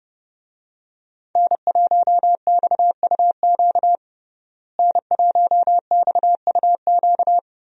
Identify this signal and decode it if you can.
Morse code, heard as a keyed tone, about 28 wpm, 700 Hz. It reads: N1XUQ N1XUQ
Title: USA Call Signs - 5 Characters - Repeated 3x - 30wpm